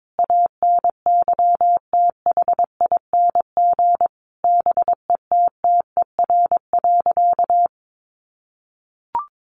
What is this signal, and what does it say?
Morse code, heard as a keyed tone, about 22 wpm, 700 Hz. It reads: ANYTHING BETTER.
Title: Sentences from Top 1100 Words 22wpm